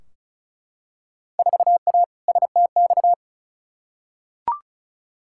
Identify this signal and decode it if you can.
Morse code, heard as a keyed tone, about 34 wpm, 700 Hz. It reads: 4A STX
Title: Field Day Exchanges - 35wpm